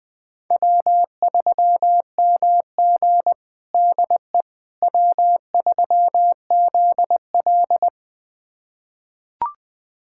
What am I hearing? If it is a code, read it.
Morse code, heard as a keyed tone, about 21 wpm, 700 Hz. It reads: W3MG DE W3ZL
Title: USA Callsign DE Callsign - 4 Characters - 20wpm